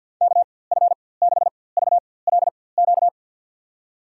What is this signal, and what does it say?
Morse code, heard as a keyed tone, about 51 wpm, spaced as for 23 wpm, 700 Hz. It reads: KF6VLQ
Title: USA Call Signs - 6 Characters - 4x Character Spacing - 50wpm